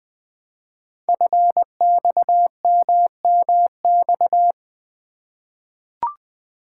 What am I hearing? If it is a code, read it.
Morse code, heard as a keyed tone, about 20 wpm, 700 Hz. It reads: FXMMX
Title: Code Groups - 5 Letters - 20wpm